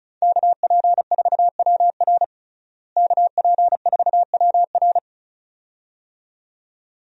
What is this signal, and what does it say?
Morse code, heard as a keyed tone, about 36 wpm, 700 Hz. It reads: KP4WR KP4WR
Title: USA Call Signs - 5 Characters - Repeated 2x - 35wpm